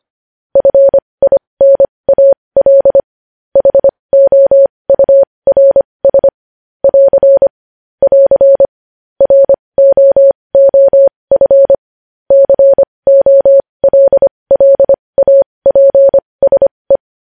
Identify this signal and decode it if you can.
Morse code, heard as a keyed tone, about 25 wpm, spaced as for 20 wpm, 550 Hz. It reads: FINAL HOURS <AR> <AR> ROOF COLLAPSE
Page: news headlines in morse code at 20 wpm